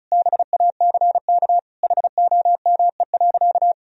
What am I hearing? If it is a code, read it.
Morse code, heard as a keyed tone, about 34 wpm, 700 Hz. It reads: BACK HOME.
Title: Sentences from Top 500 Words and Abbreviations 35wpm